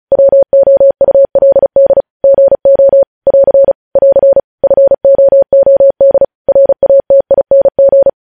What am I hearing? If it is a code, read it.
Morse code, heard as a keyed tone, about 36 wpm, 550 Hz. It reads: WOULD GO <AR> <AR> FOOD RATING